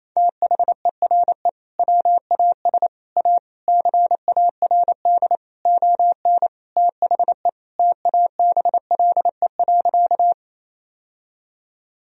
Morse code, heard as a keyed tone, about 28 wpm, 700 Hz. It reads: THERE WAS A CARD ON THE TABLE.
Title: Sentences from Top 1000 Words - Review Part 4 - 28wpm